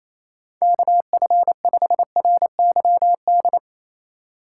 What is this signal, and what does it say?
Morse code, heard as a keyed tone, about 28 wpm, 700 Hz. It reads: KF5RYD
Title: USA Call Signs - 6 Characters - 6x Character Spacing - 28wpm